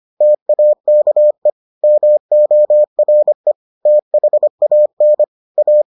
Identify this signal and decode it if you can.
Morse code, heard as a keyed tone, about 25 wpm, 600 Hz. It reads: TAKE MORE THAN A